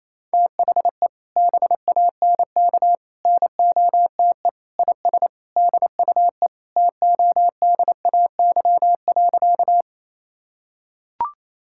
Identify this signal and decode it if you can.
Morse code, heard as a keyed tone, about 28 wpm, 700 Hz. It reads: THE BANK NOTE IS DUE TODAY.
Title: Sentences from Top 1000 Words - Review Part 6 - 28wpm